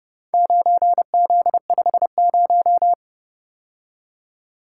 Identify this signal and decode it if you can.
Morse code, heard as a keyed tone, about 29 wpm, 700 Hz. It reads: 9Z50